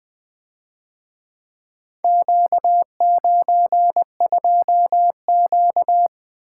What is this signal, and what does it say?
Morse code, heard as a keyed tone, about 23 wpm, 700 Hz. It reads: Q92Q
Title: Speed-Racing - Four Letter-Number Combinations - 20wpm